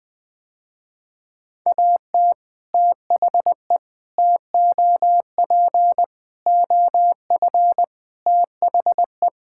Morse code, heard as a keyed tone, about 20 wpm, 700 Hz. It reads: AT THE TOP OF THE
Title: Sentences with 3-Word Phrases - Spoken Beginning - 20wpm